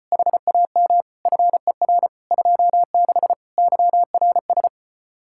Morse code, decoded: HAM FER 26 YRS

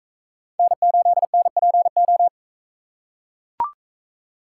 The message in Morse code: N8NPO

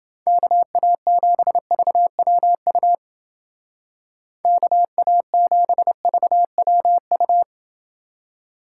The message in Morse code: KA7VWU KA7VWU

